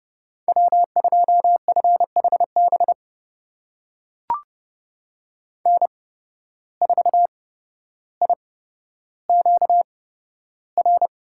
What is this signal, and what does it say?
Morse code, heard as a keyed tone, about 30 wpm, spaced as for 13 wpm, 700 Hz. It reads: W2FHB N4IQR
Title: USA Call Signs - 5 Characters - 8x Character Spacing - 30wpm